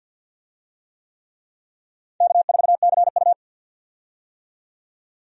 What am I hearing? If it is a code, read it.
Morse code, heard as a keyed tone, about 52 wpm, 700 Hz. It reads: K4CU
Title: USA Call Signs - 4 Characters - 50wpm